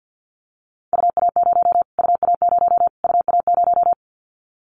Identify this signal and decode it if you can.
Morse code, heard as a keyed tone, about 50 wpm, 700 Hz. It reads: UA0 UA0 UA0
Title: Callsign Prefixes for Top 25 DXCC - Repeated 3x - 50wpm